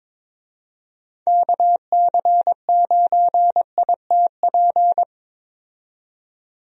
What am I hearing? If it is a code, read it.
Morse code, heard as a keyed tone, about 21 wpm, 700 Hz. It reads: KC9ITP
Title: USA Call Signs - 6 Characters - 22wpm